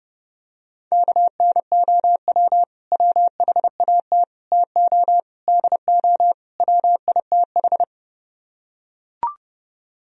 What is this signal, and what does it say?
Morse code, heard as a keyed tone, about 30 wpm, 700 Hz. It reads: KNOW WHAT TO DO WITH